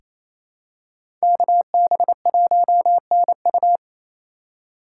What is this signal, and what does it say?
Morse code, heard as a keyed tone, about 28 wpm, 700 Hz. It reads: KB1NU
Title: USA Call Signs - 5 Characters - 4x Character Spacing - 28wpm